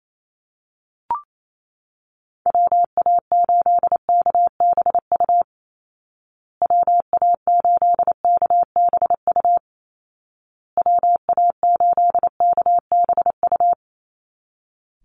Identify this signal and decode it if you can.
Morse code, heard as a keyed tone, about 28 wpm, 700 Hz. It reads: WA8KBU WA8KBU WA8KBU
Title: Call Signs Repeated 2x - 28wpm